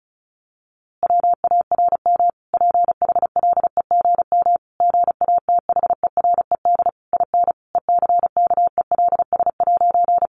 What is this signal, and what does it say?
Morse code, heard as a keyed tone, about 34 wpm, 700 Hz. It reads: WARM PHLEGM GATHERED IN ECKELS'
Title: A Sound Of Thunder 35wpm